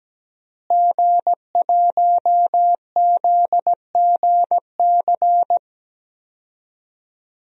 Morse code, heard as a keyed tone, about 17 wpm, 700 Hz. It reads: G1ZGC
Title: DXCC-100 Call Signs - 5 Characters - Repeated 3x - 17wpm